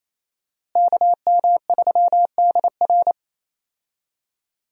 Morse code, heard as a keyed tone, about 28 wpm, 700 Hz. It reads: KM3DR
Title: USA Call Signs - 5 Characters - Repeated 3x - 28wpm